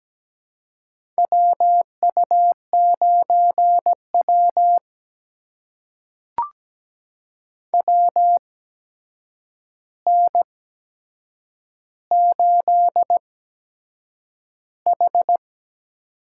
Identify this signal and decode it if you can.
Morse code, heard as a keyed tone, about 17 wpm, spaced as for 7 wpm, 700 Hz. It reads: WU9W WN8H
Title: USA Call Signs - 4 Characters - 8x Character Spacing - 17wpm